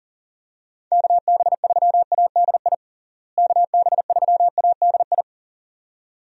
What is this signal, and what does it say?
Morse code, heard as a keyed tone, about 40 wpm, 700 Hz. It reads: KB3ADI KB3ADI
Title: USA Call Signs - 6 Characters - Repeated 2x - 40wpm